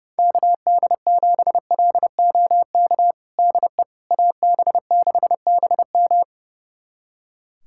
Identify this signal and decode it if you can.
Morse code, heard as a keyed tone, about 30 wpm, 700 Hz. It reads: KD7LOK DE AB6BM
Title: USA Callsign DE Callsign 30wpm